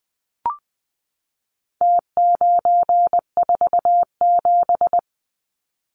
Morse code, heard as a keyed tone, about 20 wpm, 700 Hz. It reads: T947
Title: Numbers - Contest Style - 4-Digit 20wpm